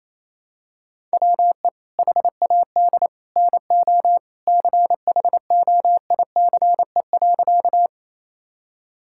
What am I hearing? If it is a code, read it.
Morse code, heard as a keyed tone, about 28 wpm, 700 Hz. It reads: WE HAD NO CHOICE.
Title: Sentences from Top 1000 Words - Review Part 6 - 28wpm